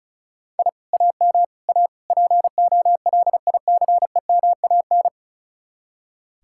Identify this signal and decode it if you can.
Morse code, heard as a keyed tone, about 45 wpm, 700 Hz. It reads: I AM A POLICEMAN